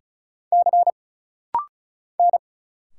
Morse code, heard as a keyed tone, about 36 wpm, 700 Hz. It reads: C N
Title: Single Letters - Rapid-Fire 35wpm